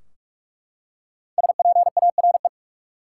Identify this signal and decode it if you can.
Morse code, heard as a keyed tone, about 44 wpm, 700 Hz. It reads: SPARE
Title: Five Letter Words - 45wpm